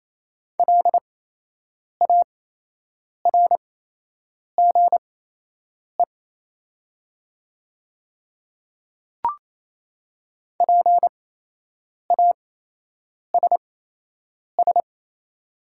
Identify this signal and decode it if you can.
Morse code, heard as a keyed tone, about 28 wpm, spaced as for 8 wpm, 700 Hz. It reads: LARGE PASS
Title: 8x Character Spacing - Top 500 Words - 28wpm